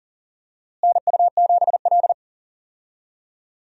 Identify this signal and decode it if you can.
Morse code, heard as a keyed tone, about 40 wpm, 700 Hz. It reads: NU7L
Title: USA Call Signs - 4 Characters - Repeated 3x - 40wpm